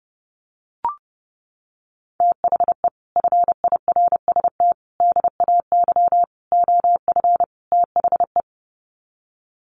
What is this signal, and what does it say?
Morse code, heard as a keyed tone, about 28 wpm, 700 Hz. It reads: THE FIRST DAY OF THE
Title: Sets of 5 Words Encore2 30wpm